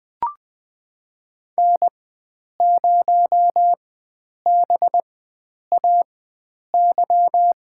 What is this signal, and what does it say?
Morse code, heard as a keyed tone, about 20 wpm, spaced as for 9 wpm, 700 Hz. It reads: N0BAY
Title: USA Call Signs - 5 Characters - 4x Character Spacing - 20wpm